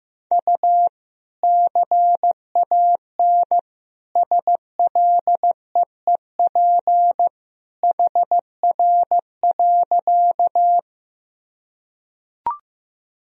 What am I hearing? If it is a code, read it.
Morse code, heard as a keyed tone, about 15 wpm, 700 Hz. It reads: U CAN SLEEP HR.
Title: Sentences from Top 500 Words and Abbreviations 15wpm